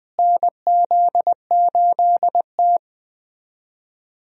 Morse code, heard as a keyed tone, about 20 wpm, 700 Hz. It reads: NZ8T
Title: USA Call Signs - 4 Characters - Repeated 3x - 20wpm